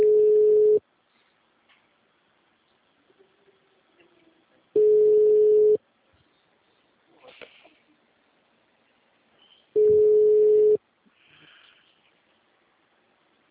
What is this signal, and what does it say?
Ringback tone (Europe, 425 Hz).
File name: out- -2005- - - .45